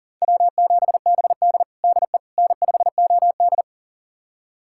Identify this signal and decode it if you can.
Morse code, heard as a keyed tone, about 40 wpm, 700 Hz. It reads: W7BD DE N5OD
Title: USA Callsign DE Callsign - 4 Characters - 40wpm